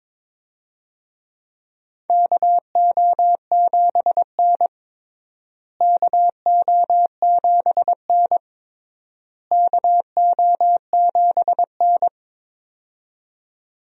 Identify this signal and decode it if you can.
Morse code, heard as a keyed tone, about 22 wpm, 700 Hz. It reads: KO7N KO7N KO7N
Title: USA Call Signs - 4 Characters - Repeated 3x - 22wpm